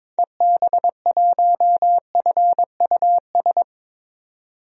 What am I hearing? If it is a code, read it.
Morse code, heard as a keyed tone, about 22 wpm, 700 Hz. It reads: EB1FUS
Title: DXCC-100 Call Signs - 6 Characters - 22wpm